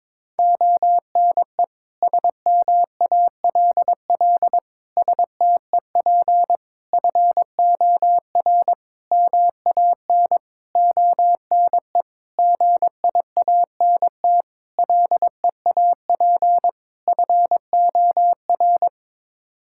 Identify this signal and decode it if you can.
Morse code, heard as a keyed tone, about 22 wpm, 700 Hz. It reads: ONE SMALL STEP FOR MAN ONE GIANT LEAP FOR
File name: Sets of 9 Words - 22wpm